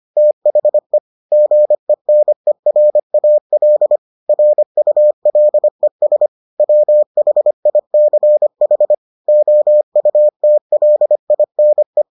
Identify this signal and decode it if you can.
Morse code, heard as a keyed tone, about 25 wpm, 600 Hz. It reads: THE GENERAL RULES WHICH OUTLINE